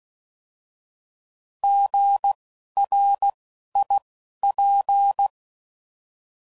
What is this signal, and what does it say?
Morse code, heard as a keyed tone, about 16 wpm, spaced as for 11 wpm, 800 Hz. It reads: GRIP
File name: Prefixes / By First Letter